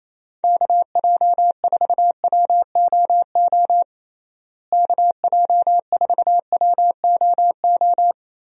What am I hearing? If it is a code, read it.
Morse code, heard as a keyed tone, about 28 wpm, 700 Hz. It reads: KJ4WOO KJ4WOO
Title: USA Call Signs - 6 Characters - Repeated 3x - 28wpm